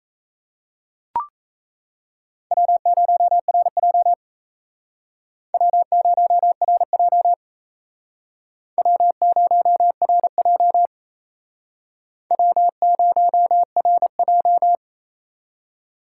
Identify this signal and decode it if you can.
Morse code, heard as a keyed tone, about 35 wpm, 700 Hz. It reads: W0RJ W0RJ W0RJ W0RJ